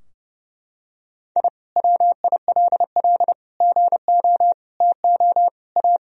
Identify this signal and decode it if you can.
Morse code, heard as a keyed tone, about 30 wpm, 700 Hz. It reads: I WILL GO TO A
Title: Sentences from Top 1000 Words - Review Part 4 - 30wpm